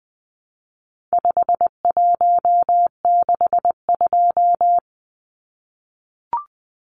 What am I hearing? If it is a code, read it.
Morse code, heard as a keyed tone, about 20 wpm, 700 Hz. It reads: 5162
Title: Numbers - 4-Digit 20wpm